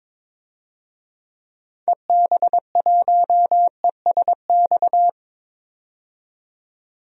Morse code, heard as a keyed tone, about 22 wpm, 700 Hz. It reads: EB1ESX
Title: DXCC-100 Call Signs - 6 Characters - 22wpm